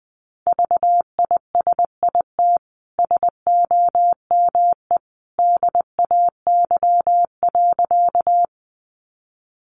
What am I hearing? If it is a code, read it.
Morse code, heard as a keyed tone, about 20 wpm, 700 Hz. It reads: VISIT SOME DAY.
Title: Sentences from Top 900 Words 20wpm